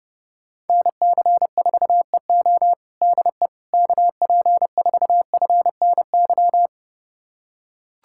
Morse code, heard as a keyed tone, about 32 wpm, 700 Hz. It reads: NC4EO DE KP4FNY
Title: USA Callsign DE Callsign 30wpm